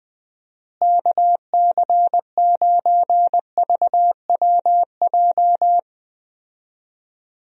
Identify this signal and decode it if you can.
Morse code, heard as a keyed tone, about 20 wpm, 700 Hz. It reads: KC9VWJ
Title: USA Call Signs - 6 Characters - Repeated 3x - 20wpm